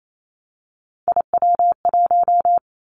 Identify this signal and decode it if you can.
Morse code, heard as a keyed tone, about 28 wpm, 700 Hz. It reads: IW1
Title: Callsign Prefixes for Top 25 DXCC - Repeated 3x - 28wpm